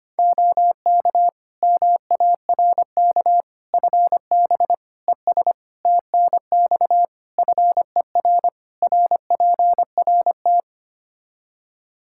Morse code, heard as a keyed tone, about 25 wpm, 700 Hz. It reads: OK MARK FB ES TNX FER RPRT
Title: Practice QSO - Exchange 2 - Info - 25wpm